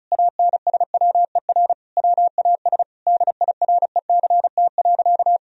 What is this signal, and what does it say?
Morse code, heard as a keyed tone, about 36 wpm, 700 Hz. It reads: ANSWER WAS DIRECT.
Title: Sentences from Top 1000 Words - Review Part 2 - 35wpm